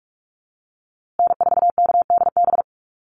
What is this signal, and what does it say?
Morse code, heard as a keyed tone, about 44 wpm, 700 Hz. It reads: N4KDB
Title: USA Call Signs Repeated 2x 45wpm